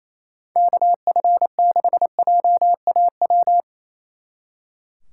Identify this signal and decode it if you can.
Morse code, heard as a keyed tone, about 34 wpm, 700 Hz. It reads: KF6JAW
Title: Speed Racing - USA 2x3 Call Signs 28wpm